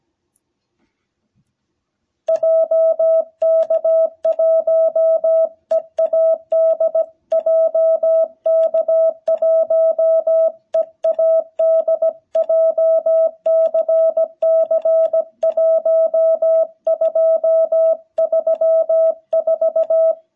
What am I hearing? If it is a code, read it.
Morse code, heard as a keyed tone, about 17 wpm, 660 Hz. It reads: JK1EAD JK1EAD JCC1234
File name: jh8cbh / 2021cw